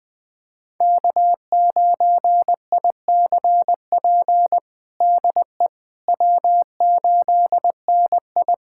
Morse code, heard as a keyed tone, about 20 wpm, 700 Hz. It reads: K9ICP DE W8NI
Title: POTA QSO Practice - 20wpm